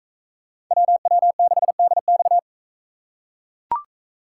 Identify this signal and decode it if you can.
Morse code, heard as a keyed tone, about 42 wpm, 700 Hz. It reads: WW6DX